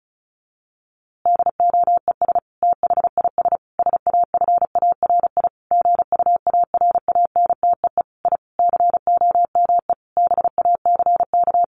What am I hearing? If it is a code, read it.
Morse code, heard as a keyed tone, about 34 wpm, 700 Hz. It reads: DOES THIS SAFARI GUARANTEE I COME BACK
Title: A Sound Of Thunder 35wpm